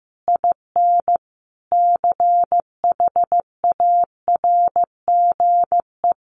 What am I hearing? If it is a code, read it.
Morse code, heard as a keyed tone, about 15 wpm, 700 Hz. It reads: IN CHARGE
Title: Sentences from Top 1200 Words 15wpm